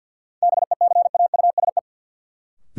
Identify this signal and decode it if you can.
Morse code, heard as a keyed tone, about 50 wpm, 700 Hz. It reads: BECAUSE
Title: Sets of 2 Words - New Word Drill - 50wpm